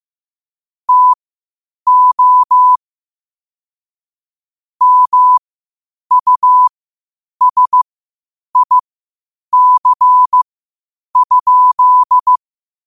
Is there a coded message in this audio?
Yes: Morse code from 0.9 to 12.4 s, heard as a keyed tone, about 15 wpm, spaced as for 8 wpm, 1000 Hz. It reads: TO MUSIC?